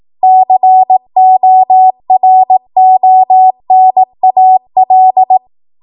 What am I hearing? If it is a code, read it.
Morse code, heard as a keyed tone, about 18 wpm, 750 Hz. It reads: CORONAL
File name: Morse / 18WPM